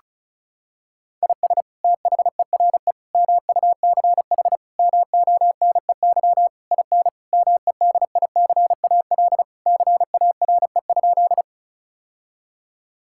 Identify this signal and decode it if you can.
Morse code, heard as a keyed tone, about 36 wpm, 700 Hz. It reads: IS THERE MUCH MONEY IN MEDICAL CARE?
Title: Sentences from Top 1000 Words - Review Part 5 - 35wpm